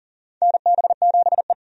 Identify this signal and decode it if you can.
Morse code, heard as a keyed tone, about 39 wpm, 700 Hz. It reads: NB7E
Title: USA Call Signs - 4 Characters - 40wpm